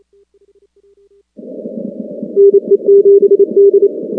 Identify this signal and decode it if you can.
Morse code, heard as a keyed tone, about 28 wpm, 400 Hz. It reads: NE7D
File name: contests / 2019-cw-wpx-kt5j-k5pi- -2330 0-19